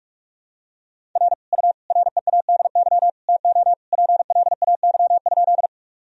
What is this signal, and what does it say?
Morse code, heard as a keyed tone, about 41 wpm, 700 Hz. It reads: R U READY TO PLAY?
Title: Sentences from Top 400 Words and Abbreviations 45wpm